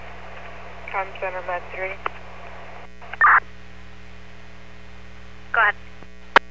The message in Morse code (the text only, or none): none